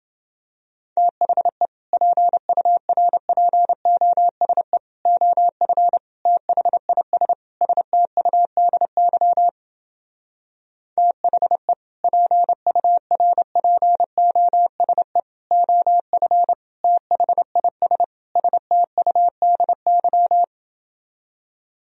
Morse code, heard as a keyed tone, about 29 wpm, 700 Hz. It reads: THE PURPOSE OF THIS STUDY THE PURPOSE OF THIS STUDY